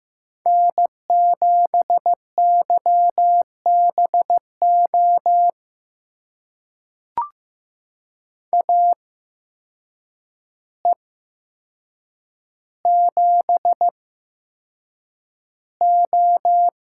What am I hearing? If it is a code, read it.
Morse code, heard as a keyed tone, about 15 wpm, spaced as for 7 wpm, 700 Hz. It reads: N7YBO AE7O
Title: USA Call Signs - 5 Characters - 8x Character Spacing - 15wpm